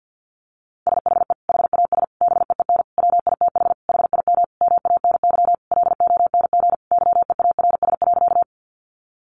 Morse code, heard as a keyed tone, about 50 wpm, 700 Hz. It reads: SHE HAS BEEN WITH HIM MANY LONG YEARS.
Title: Sentences from Top 200 Words 50wpm